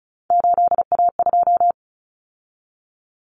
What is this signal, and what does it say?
Morse code, heard as a keyed tone, about 34 wpm, 700 Hz. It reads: 8A2
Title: Common Cut Numbers - 3-Digit 35wpm